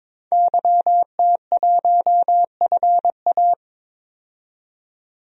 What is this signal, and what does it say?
Morse code, heard as a keyed tone, about 21 wpm, 700 Hz. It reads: YT1FA
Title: DXCC-100 Call Signs - 5 Characters - 22wpm